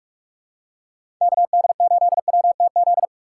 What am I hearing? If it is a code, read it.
Morse code, heard as a keyed tone, about 42 wpm, 700 Hz. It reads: KD8WTZ
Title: USA Call Signs - 6 Characters - 4x Character Spacing - 45wpm